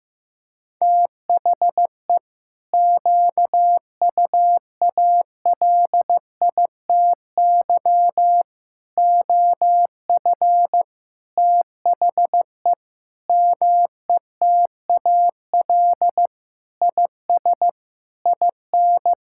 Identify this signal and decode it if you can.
Morse code, heard as a keyed tone, about 16 wpm, 700 Hz. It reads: THE QUALITY OF THE METAL IS IN
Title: Sentences from Top 1000 Words - Review Part 6 - 15wpm